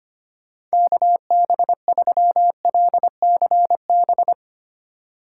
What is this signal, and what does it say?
Morse code, heard as a keyed tone, about 25 wpm, 700 Hz. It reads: KB3LCB
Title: USA Call Signs - 6 Characters - 3x Character Spacing - 25wpm